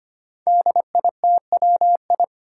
Morse code, heard as a keyed tone, about 25 wpm, 700 Hz. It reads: DITWI
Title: Five Challenging Letter-Number Combinations 25wpm